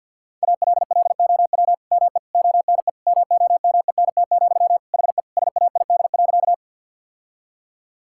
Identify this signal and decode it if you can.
Morse code, heard as a keyed tone, about 50 wpm, 700 Hz. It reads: ALLOW ME ONE MOMENT, HE SAID.
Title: Sentences from Top 1000 Words - Review Part 5 - 50wpm